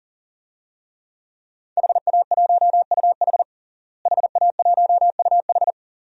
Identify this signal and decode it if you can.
Morse code, heard as a keyed tone, about 40 wpm, 700 Hz. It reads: HA1UH HA1UH
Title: DXCC-100 Call Signs - 5 Characters - Repeated 3x - 40wpm